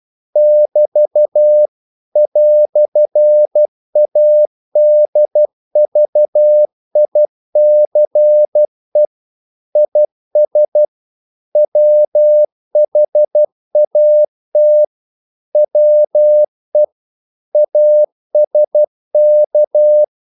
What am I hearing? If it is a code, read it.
Morse code, heard as a keyed tone, about 12 wpm, 600 Hz. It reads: <BT> "ADVICE IS WHAT WE ASK
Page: English quotations in CW (telegraphy - morse)